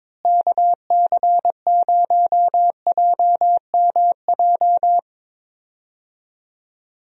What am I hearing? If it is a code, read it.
Morse code, heard as a keyed tone, about 22 wpm, 700 Hz. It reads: KC0JMJ
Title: USA Call Signs - 6 Characters - Repeated 3x - 22wpm